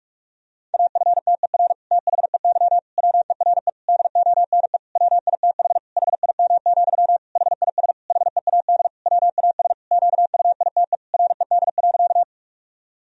Morse code, heard as a keyed tone, about 46 wpm, 700 Hz. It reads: AFTER THEY WERE DONE WITH HIM, HIS HEAD WAS QUITE RED.
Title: Sentences from Top 1000 Words - Review Part 2 - 45wpm